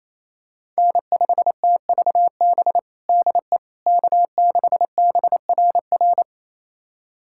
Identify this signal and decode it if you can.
Morse code, heard as a keyed tone, about 28 wpm, 700 Hz. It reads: N5TVB DE K6BRR
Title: USA Callsign DE Callsign - 5 Characters - 28wpm